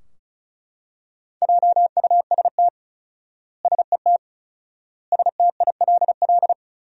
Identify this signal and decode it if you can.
Morse code, heard as a keyed tone, about 34 wpm, 700 Hz. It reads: JUST SET STILL
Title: 4x Word-Spacing - Sentences from Top 300 Words - 35wpm